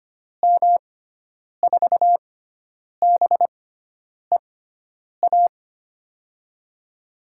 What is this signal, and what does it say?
Morse code, heard as a keyed tone, about 25 wpm, spaced as for 9 wpm, 700 Hz. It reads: M4BEA
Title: USA Call Signs - 6 Characters - 6x Character Spacing - 25wpm